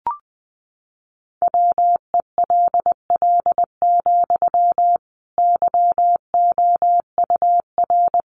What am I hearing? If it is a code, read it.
Morse code, heard as a keyed tone, about 20 wpm, 700 Hz. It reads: WELL, YOUR
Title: Sentences from Top 1400 Words 20wpm